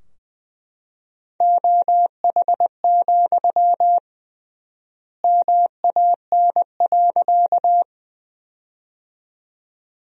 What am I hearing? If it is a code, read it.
Morse code, heard as a keyed tone, about 20 wpm, 700 Hz. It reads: OH, MAN.
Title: 3x Word-Spacing - Sentences from Top 500 Words - 20wpm